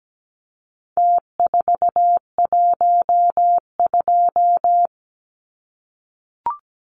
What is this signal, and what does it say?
Morse code, heard as a keyed tone, about 17 wpm, 700 Hz. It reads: T412
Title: Numbers - Contest Style - 4-Digit 17wpm